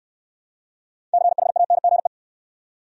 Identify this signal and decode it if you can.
Morse code, heard as a keyed tone, about 67 wpm, 700 Hz. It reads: CHANGE